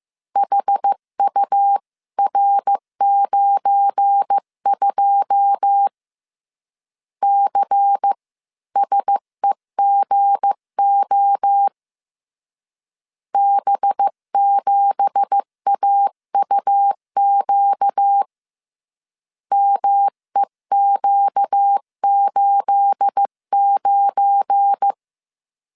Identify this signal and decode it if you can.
Morse code, heard as a keyed tone, about 15 wpm, 800 Hz. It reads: HUR92 CSEGO B7AUQ MEQ89
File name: CW / DET / 1 9 1500C S B 14wpm 3302